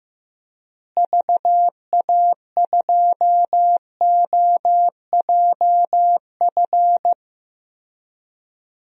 Morse code, heard as a keyed tone, about 15 wpm, 700 Hz. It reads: VA2OJF
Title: DXCC-100 Call Signs - 6 Characters - Repeated 3x - 15wpm